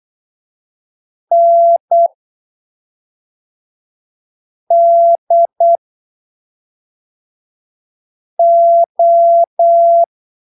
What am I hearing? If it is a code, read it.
Morse code, heard as a keyed tone, about 8 wpm, spaced as for 3 wpm, 680 Hz. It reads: NDO
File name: arquivos / texto / texto 13 08ppm